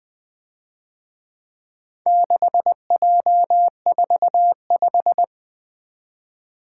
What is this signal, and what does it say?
Morse code, heard as a keyed tone, about 23 wpm, 700 Hz. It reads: 6J45
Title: Speed-Racing - Four Letter-Number Combinations - 20wpm